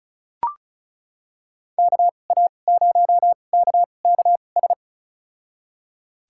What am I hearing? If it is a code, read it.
Morse code, heard as a keyed tone, about 34 wpm, spaced as for 25 wpm, 700 Hz. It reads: KA0KKS
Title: USA Call Signs - 6 Characters - 2x Character Spacing - 35wpm